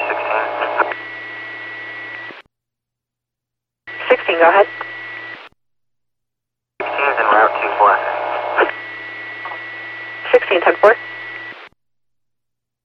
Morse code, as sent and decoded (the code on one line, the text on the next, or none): none